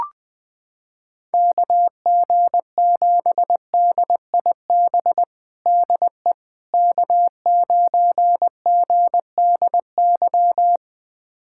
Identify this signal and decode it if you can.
Morse code, heard as a keyed tone, about 21 wpm, 700 Hz. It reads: KG7DIB DE K9GDY
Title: USA Callsign DE Callsign 20wpm